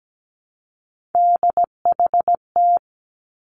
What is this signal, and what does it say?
Morse code, heard as a keyed tone, about 17 wpm, 700 Hz. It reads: DHT
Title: Three Challenging Letter-Number Combinations 17wpm